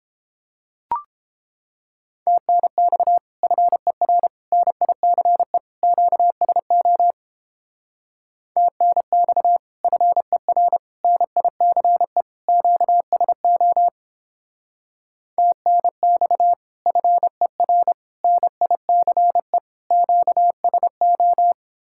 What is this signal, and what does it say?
Morse code, heard as a keyed tone, about 31 wpm, 700 Hz. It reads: TNX FER NICE QSO TNX FER NICE QSO TNX FER NICE QSO